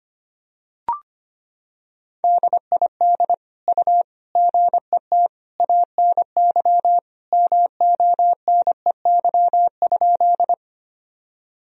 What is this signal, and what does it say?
Morse code, heard as a keyed tone, about 25 wpm, 700 Hz. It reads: DID U GET ANY MONEY?
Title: Sentences from Top 400 Words and Abbreviations 25wpm